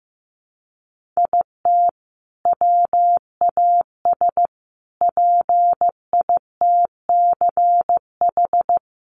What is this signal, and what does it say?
Morse code, heard as a keyed tone, about 14 wpm, 700 Hz. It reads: IT WAS PITCH
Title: Sentences from Top 1300 Words 15wpm